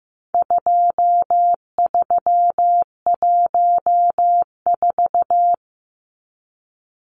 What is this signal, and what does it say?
Morse code, heard as a keyed tone, about 15 wpm, 700 Hz. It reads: 2314
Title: Numbers - 4-Digit 15wpm